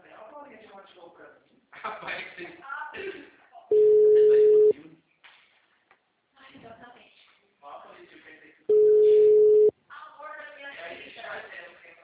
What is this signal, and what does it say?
Ringback tone (Europe, 425 Hz), with speech.
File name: out- -2003- - - .22